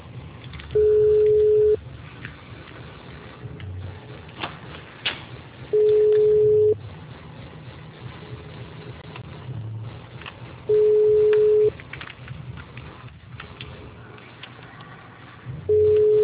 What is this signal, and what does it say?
Ringback tone (Europe, 425 Hz).